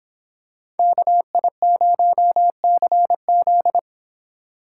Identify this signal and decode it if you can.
Morse code, heard as a keyed tone, about 25 wpm, 700 Hz. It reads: KI0CZ